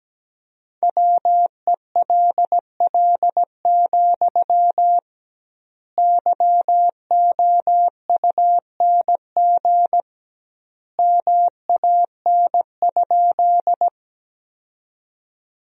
Morse code, heard as a keyed tone, about 17 wpm, 700 Hz. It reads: WELL, YOUNG MAN?
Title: 2x Word-Spacing - Sentences from Top 400 Words - 17wpm